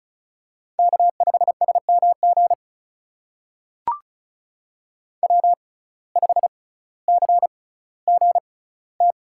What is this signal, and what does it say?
Morse code, heard as a keyed tone, about 36 wpm, spaced as for 18 wpm, 700 Hz. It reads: K5SMG W5CGT
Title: USA Call Signs - 5 Characters - 6x Character Spacing - 35wpm